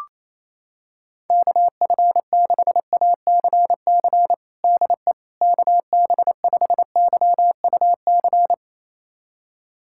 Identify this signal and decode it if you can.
Morse code, heard as a keyed tone, about 28 wpm, 700 Hz. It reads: KF6ACC DE KB5YUC
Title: USA Callsign DE Callsign 28wpm